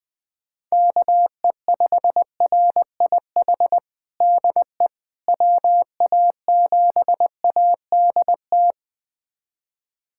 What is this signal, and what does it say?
Morse code, heard as a keyed tone, about 21 wpm, 700 Hz. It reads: KE5RIH DE WA7ADT
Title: USA Callsign DE Callsign - 6 Characters - 20wpm